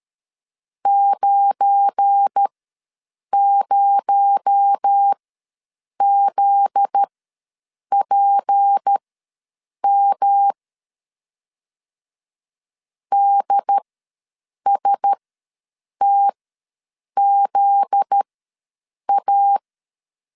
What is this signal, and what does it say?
Morse code, heard as a keyed tone, about 13 wpm, spaced as for 7 wpm, 800 Hz. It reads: 90ZPM DSTZA
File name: 7 GF9Z ws 12wpm